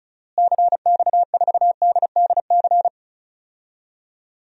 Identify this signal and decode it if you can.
Morse code, heard as a keyed tone, about 34 wpm, 700 Hz. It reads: CX4DDC
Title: DXCC-100 Call Signs - 6 Characters - Repeated 3x - 35wpm